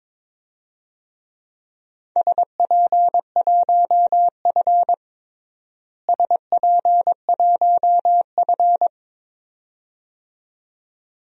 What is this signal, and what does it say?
Morse code, heard as a keyed tone, about 22 wpm, 700 Hz. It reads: SP1F SP1F
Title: DXCC-100 Call Signs - 4 Characters - Repeated 2x - 22wpm